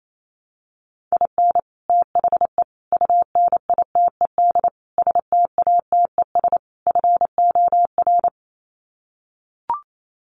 Morse code, heard as a keyed tone, about 28 wpm, 700 Hz. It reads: IN THE UNITED STATES FOR
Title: Sets of 5 Words Encore2 28wpm